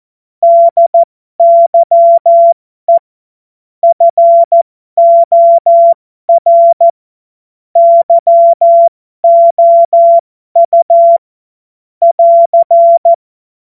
Morse code, heard as a keyed tone, about 14 wpm, spaced as for 12 wpm, 680 Hz. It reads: DYE FOR YOU <AR>
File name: Morsekurs / Witze